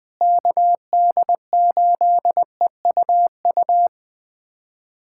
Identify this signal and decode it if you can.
Morse code, heard as a keyed tone, about 20 wpm, 700 Hz. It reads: KD8EUU
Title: USA Callsign DE Callsign - 6 Characters - 20wpm